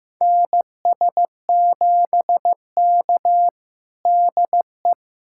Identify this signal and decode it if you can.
Morse code, heard as a keyed tone, about 16 wpm, 700 Hz. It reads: NS7K DE
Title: USA Callsign DE Callsign - 4 Characters - 15wpm